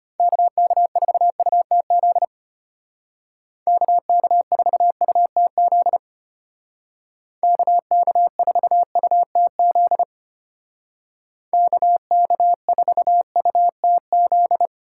Morse code, heard as a keyed tone, about 33 wpm, 700 Hz. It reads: KK4UTZ KK4UTZ KK4UTZ KK4UTZ